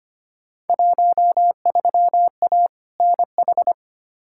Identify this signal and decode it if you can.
Morse code, heard as a keyed tone, about 25 wpm, 700 Hz. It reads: 13A NH
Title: Field Day Exchanges Encore - 25wpm